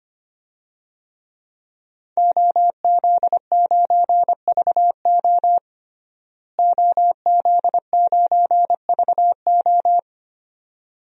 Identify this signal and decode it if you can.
Morse code, heard as a keyed tone, about 25 wpm, 700 Hz. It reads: OZ9VO OZ9VO
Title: DXCC-100 Call Signs - 5 Characters - Repeated 2x - 25wpm